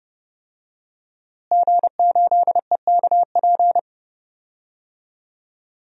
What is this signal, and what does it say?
Morse code, heard as a keyed tone, about 30 wpm, 700 Hz. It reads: G8EKP
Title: DXCC-100 Call Signs - 5 Characters - Repeated 3x - 30wpm